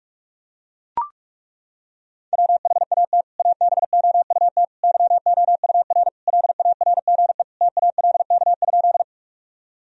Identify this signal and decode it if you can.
Morse code, heard as a keyed tone, about 42 wpm, 700 Hz. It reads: WHAT ABOUT YOUR LARGE TALK?
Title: Sentences from Top 1000 Words - Review Part 1 - 45wpm